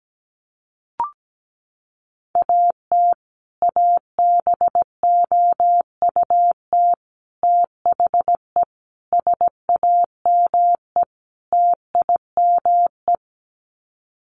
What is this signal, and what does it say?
Morse code, heard as a keyed tone, about 17 wpm, 700 Hz. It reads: AT ABOUT THE SAME TIME
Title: Sets of 5 Words Encore 17wpm